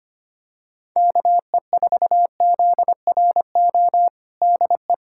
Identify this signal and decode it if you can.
Morse code, heard as a keyed tone, about 25 wpm, 700 Hz. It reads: KE4ZRO DE
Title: USA Callsign DE Callsign 25wpm